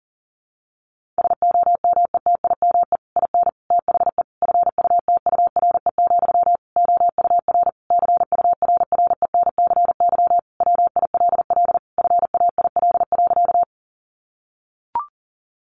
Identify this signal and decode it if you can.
Morse code, heard as a keyed tone, about 40 wpm, 700 Hz. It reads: SOMETIME IN THE FUTURE, OUR CURRENCY WILL FAIL.
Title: Sentences from Top 1500 Words 40wpm